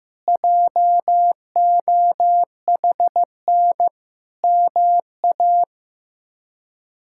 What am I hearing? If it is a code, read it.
Morse code, heard as a keyed tone, about 15 wpm, 700 Hz. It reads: JOHN MA
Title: K1USN SST QSO Practice - 15wpm